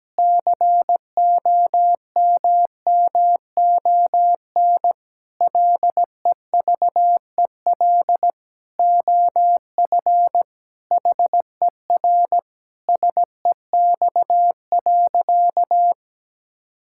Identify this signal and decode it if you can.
Morse code, heard as a keyed tone, about 17 wpm, 700 Hz. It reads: COMMON LEVEL OF HER SEX.
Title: Sentences from Top 1600 Words 17wpm